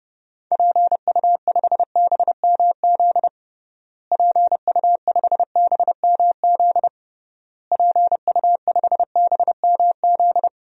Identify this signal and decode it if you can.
Morse code, heard as a keyed tone, about 29 wpm, 700 Hz. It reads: PU5BMZ PU5BMZ PU5BMZ
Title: DXCC-100 Call Signs - 6 Characters - Repeated 3x - 30wpm